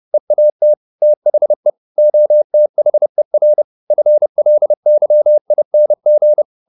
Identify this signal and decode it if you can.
Morse code, heard as a keyed tone, about 30 wpm, 600 Hz. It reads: EAT THE OTHER FLYING